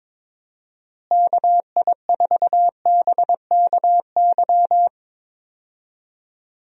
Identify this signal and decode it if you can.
Morse code, heard as a keyed tone, about 22 wpm, 700 Hz. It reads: KI4BKY
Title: USA Call Signs - 6 Characters - Repeated 2x - 22wpm